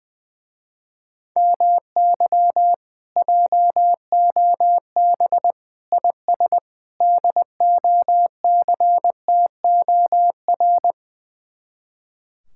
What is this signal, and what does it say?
Morse code, heard as a keyed tone, about 20 wpm, 700 Hz. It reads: MY JOB IS DOCTOR
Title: Speed-Racing Common QSO Phrases - 20wpm